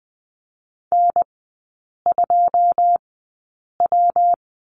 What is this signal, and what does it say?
Morse code, heard as a keyed tone, about 20 wpm, spaced as for 8 wpm, 700 Hz. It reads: N2W
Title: 114 Lesson 19e - Letter w - Three Characters 20wpm